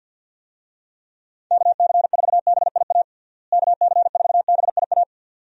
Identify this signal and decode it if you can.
Morse code, heard as a keyed tone, about 50 wpm, 700 Hz. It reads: KC4BIA KC4BIA
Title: USA Call Signs - 6 Characters - Repeated 2x - 50wpm